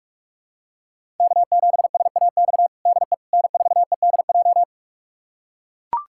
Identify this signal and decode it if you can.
Morse code, heard as a keyed tone, about 42 wpm, 700 Hz. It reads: K7SAX DE N4EDJ
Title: USA Callsign DE Callsign - 5 Characters - 45wpm